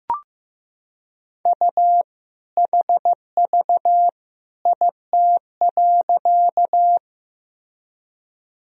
Morse code, heard as a keyed tone, about 15 wpm, 700 Hz. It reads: U HV IT.
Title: Sentences from Top 100 Words and Abbreviations 15wpm